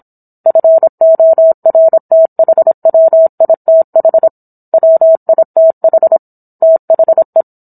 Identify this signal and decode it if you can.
Morse code, heard as a keyed tone, about 27 wpm, 650 Hz. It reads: FORTHWITH WITH THE
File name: Book One / 17-Chapter 17 - 26 WPM